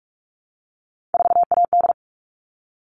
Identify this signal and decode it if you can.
Morse code, heard as a keyed tone, about 44 wpm, 700 Hz. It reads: 4AD
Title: All Cut Numbers - 3-Digit 45wpm